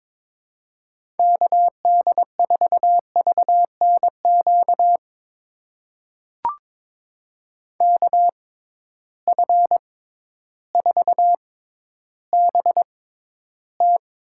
Morse code, heard as a keyed tone, about 21 wpm, spaced as for 12 wpm, 700 Hz. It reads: KD4VNQ KF4BT
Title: USA Call Signs - 6 Characters - 6x Character Spacing - 22wpm